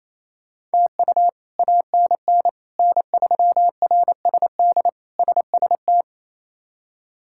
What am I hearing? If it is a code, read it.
Morse code, heard as a keyed tone, about 28 wpm, 700 Hz. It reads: TU ANN N3RSD SST
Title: K1USN SST QSO Practice - 28wpm